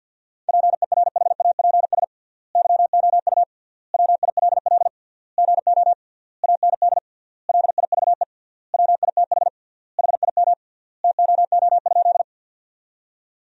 Morse code, heard as a keyed tone, about 50 wpm, 700 Hz. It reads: PERHAPS YOU WILL GO AND LIVE WITH HIM TOO?
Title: 3x Word-Spacing - Sentences from Top 500 Words - 50wpm